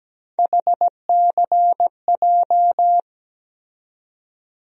Morse code, heard as a keyed tone, about 17 wpm, 700 Hz. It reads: HCJ
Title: USA Call Sign Postfixes - 3 Letters 17wpm